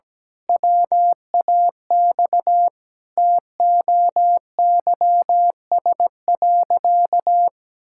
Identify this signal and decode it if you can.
Morse code, heard as a keyed tone, about 17 wpm, 700 Hz. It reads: WAX TOYS.
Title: Pangram Challenge v2 - 17wpm